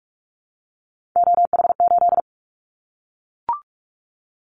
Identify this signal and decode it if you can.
Morse code, heard as a keyed tone, about 44 wpm, 700 Hz. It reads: OH8
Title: Callsign Prefixes for Top 50 DXCC - 45wpm